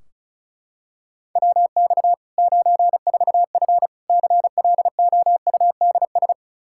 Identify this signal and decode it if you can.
Morse code, heard as a keyed tone, about 36 wpm, 700 Hz. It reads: WX 94F CLOUDS
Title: Practice QSO - Exchange 2 - Info - 35wpm